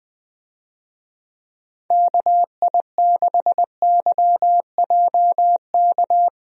Morse code, heard as a keyed tone, about 20 wpm, 700 Hz. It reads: KI6YJK
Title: USA Call Signs - 6 Characters - Repeated 3x - 20wpm